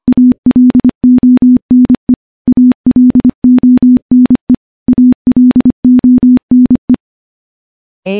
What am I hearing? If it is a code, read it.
Morse code, heard as a keyed tone, about 25 wpm, 250 Hz. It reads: ALONE ALONE ALONE
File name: morse / morse-training-25WPM